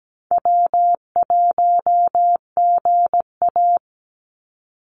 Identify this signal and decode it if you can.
Morse code, heard as a keyed tone, about 17 wpm, 700 Hz. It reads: W1GA
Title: USA Call Signs 17wpm